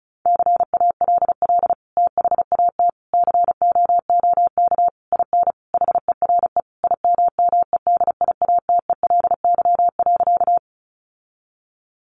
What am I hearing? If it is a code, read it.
Morse code, heard as a keyed tone, about 34 wpm, 700 Hz. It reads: CALL THAT COOK IN HERE IMMEDIATELY.
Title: Sentences from Top 1100 Words 35wpm